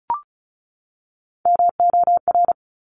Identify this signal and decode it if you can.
Morse code, heard as a keyed tone, about 36 wpm, 700 Hz. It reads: MOR
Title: Top Three Letter Combinations in Words 35wpm